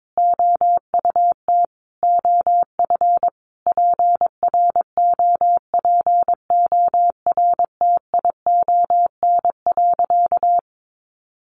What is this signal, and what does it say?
Morse code, heard as a keyed tone, about 22 wpm, 700 Hz. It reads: OUT OF PROPORTION.
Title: A Sound Of Thunder 22wpm